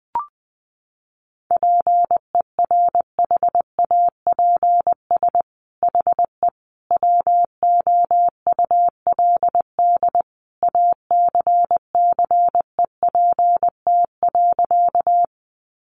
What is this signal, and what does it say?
Morse code, heard as a keyed tone, about 20 wpm, 700 Hz. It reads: PERHAPS HE WOULD ACCEPT.
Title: Sentences from Top 1300 Words 20wpm